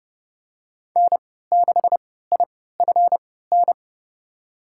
Morse code, heard as a keyed tone, about 30 wpm, spaced as for 17 wpm, 700 Hz. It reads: N6IFN
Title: USA Call Signs - 3x Character Spacing - 30wpm